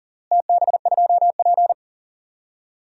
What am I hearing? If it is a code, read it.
Morse code, heard as a keyed tone, about 40 wpm, 700 Hz. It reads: TB2P
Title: DXCC-100 Call Signs - 4 Characters - Repeated 2x - 40wpm